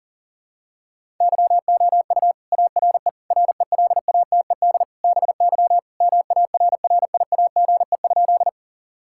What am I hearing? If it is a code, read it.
Morse code, heard as a keyed tone, about 37 wpm, 700 Hz. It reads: YOU ARE RELATED BY MARRIAGE?
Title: Sentences from Top 1000 Words - Review Part 5 - 40wpm